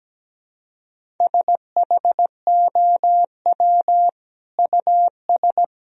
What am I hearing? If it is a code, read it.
Morse code, heard as a keyed tone, about 17 wpm, 700 Hz. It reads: SHOW US
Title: Sentences from Top 200 Words and Abbreviations 17wpm